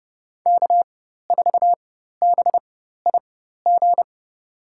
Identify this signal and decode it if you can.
Morse code, heard as a keyed tone, about 29 wpm, spaced as for 14 wpm, 700 Hz. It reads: K4BIG
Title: USA Call Signs - 5 Characters - 4x Character Spacing - 30wpm